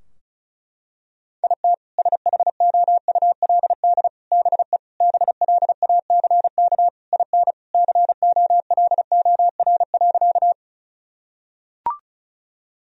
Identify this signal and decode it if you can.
Morse code, heard as a keyed tone, about 36 wpm, 700 Hz. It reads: IT SHOULD BE BLACK IN COLOR.
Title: Sentences from Top 1000 Words - Review Part 2 - 35wpm